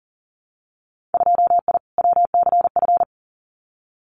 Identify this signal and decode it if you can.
Morse code, heard as a keyed tone, about 40 wpm, 700 Hz. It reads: 2I WCF
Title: Winter Field Day Exchanges 40wpm